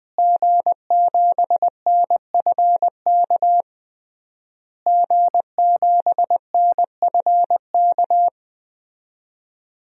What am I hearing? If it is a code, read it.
Morse code, heard as a keyed tone, about 20 wpm, 700 Hz. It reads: G7NFK G7NFK
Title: DXCC-100 Call Signs - 5 Characters - Repeated 3x - 20wpm